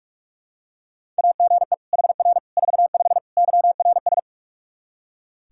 Morse code, heard as a keyed tone, about 43 wpm, 700 Hz. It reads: AGE HR 45 YRS
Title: K1USN SST QSO Practice 45wpm